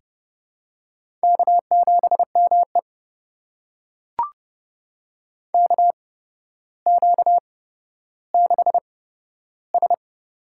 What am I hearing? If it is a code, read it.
Morse code, heard as a keyed tone, about 30 wpm, spaced as for 13 wpm, 700 Hz. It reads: K7ME KQ6S
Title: USA Call Signs - 4 Characters - 8x Character Spacing - 30wpm